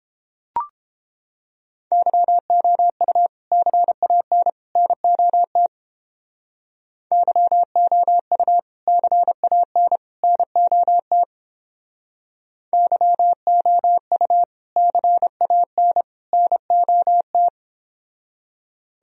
Morse code, heard as a keyed tone, about 31 wpm, 700 Hz. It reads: YOU CAN NOT YOU CAN NOT YOU CAN NOT